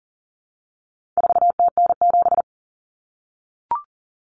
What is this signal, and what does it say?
Morse code, heard as a keyed tone, about 40 wpm, 700 Hz. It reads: 4TN7
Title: Common Cut Numbers - 4-Digit 40wpm